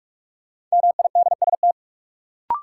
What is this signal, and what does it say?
Morse code, heard as a keyed tone, about 46 wpm, 700 Hz. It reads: MIDST
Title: Sets of 4 Words - New Word Drill - 45wpm